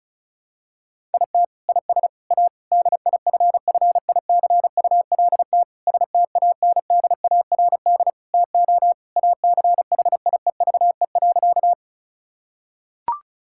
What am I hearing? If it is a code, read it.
Morse code, heard as a keyed tone, about 34 wpm, 700 Hz. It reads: IT IS A DIFFICULT STANDARD TO ACHIEVE.
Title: Sentences from Top 1000 Words - Review Part 6 - 35wpm